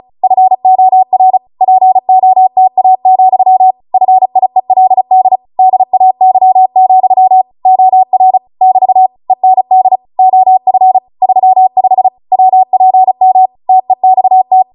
Morse code, heard as a keyed tone, about 36 wpm, 750 Hz. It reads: FOR POTA, FIELD DAY, OR <BT> END OF 35 WPM TEXT
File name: Morse / 35WPM